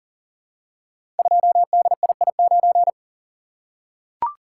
Morse code, heard as a keyed tone, about 40 wpm, 700 Hz. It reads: 2DII9
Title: Code Groups - 5 Letter:Numbers - 40wpm